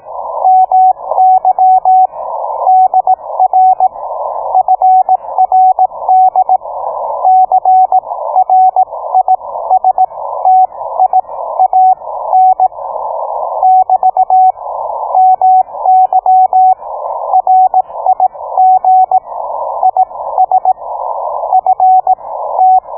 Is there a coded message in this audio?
Yes: Morse code from 0.5 to 22.8 s, heard as a keyed tone, about 18 wpm, spaced as for 15 wpm, 750 Hz. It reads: MY DR FRD CRISTIAN <BT> MY RIG IS FT